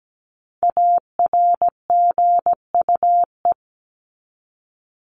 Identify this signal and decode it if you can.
Morse code, heard as a keyed tone, about 17 wpm, 700 Hz. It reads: ARGUE
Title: Top 1501-1750 Words 17wpm